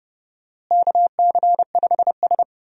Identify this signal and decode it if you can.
Morse code, heard as a keyed tone, about 32 wpm, 700 Hz. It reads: KC5S
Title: USA Call Signs - 4 Characters - Repeated 2x - 30wpm